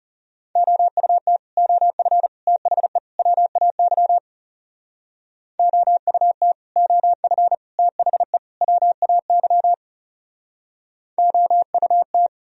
Morse code, heard as a keyed tone, about 37 wpm, 700 Hz. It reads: OUT OF THE WAY OUT OF THE WAY OUT
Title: Speed-Racing - Sets of 4 Words 30wpm